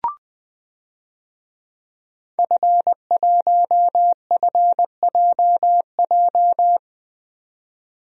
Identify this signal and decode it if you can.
Morse code, heard as a keyed tone, about 20 wpm, 700 Hz. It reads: F1FJJ
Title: DXCC-100 Call Signs - 5 Characters - 20wpm